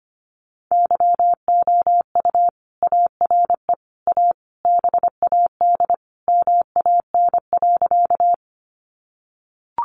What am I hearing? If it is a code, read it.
Morse code, heard as a keyed tone, about 25 wpm, 700 Hz. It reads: YOU ARE A BAD MAN.
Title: Sentences from Top 700 Words 25wpm